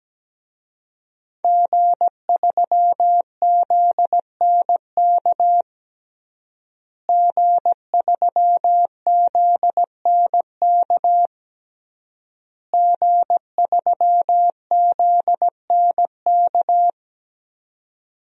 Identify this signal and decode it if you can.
Morse code, heard as a keyed tone, about 17 wpm, 700 Hz. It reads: G3ZNK G3ZNK G3ZNK
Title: DXCC-100 Call Signs - 5 Characters - Repeated 3x - 17wpm